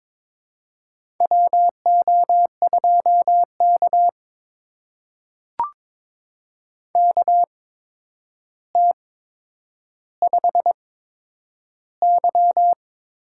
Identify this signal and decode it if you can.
Morse code, heard as a keyed tone, about 22 wpm, spaced as for 9 wpm, 700 Hz. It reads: WO2K KT5Y
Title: USA Call Signs - 8x Character Spacing - 22wpm